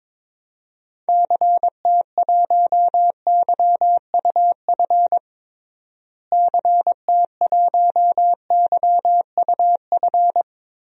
Morse code, heard as a keyed tone, about 21 wpm, 700 Hz. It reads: CT1YUF CT1YUF
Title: DXCC-100 Call Signs - 6 Characters - Repeated 3x - 22wpm